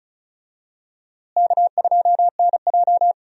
Morse code, heard as a keyed tone, about 36 wpm, 700 Hz. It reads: K2NJ
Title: USA Call Signs - 4 Characters - Repeated 2x - 35wpm